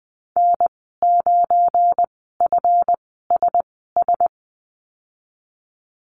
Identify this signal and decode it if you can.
Morse code, heard as a keyed tone, about 20 wpm, spaced as for 14 wpm, 700 Hz. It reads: N9FSS
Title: 179 Lesson 29g - Slash - Call Signs 20wpm